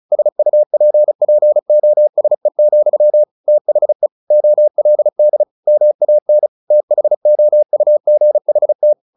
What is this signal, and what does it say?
Morse code, heard as a keyed tone, about 34 wpm, 600 Hz. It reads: SUPPOSE, THE OLD MAN THOUGHT
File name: the old man and the sea / the old man and the sea 1 35wpm